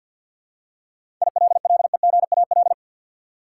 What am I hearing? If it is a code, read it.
Morse code, heard as a keyed tone, about 47 wpm, 700 Hz. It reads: ILLEGAL
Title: Seven Letter Words - 50wpm